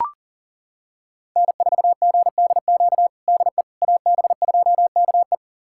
Morse code, heard as a keyed tone, about 40 wpm, 700 Hz. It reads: N4GDQ DE AB2KE
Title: USA Callsign DE Callsign - 5 Characters - 40wpm